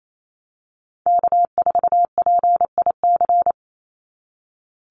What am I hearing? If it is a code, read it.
Morse code, heard as a keyed tone, about 28 wpm, 700 Hz. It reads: K4PIC
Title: NA SOTA Chaser Callsigns 28wpm